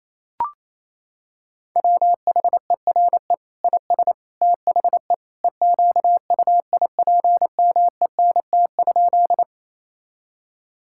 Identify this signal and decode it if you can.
Morse code, heard as a keyed tone, about 27 wpm, 700 Hz. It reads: WHERE IS THE EQUIPMENT?
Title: Sentences from Top 1000 Words - Review Part 5 - 28wpm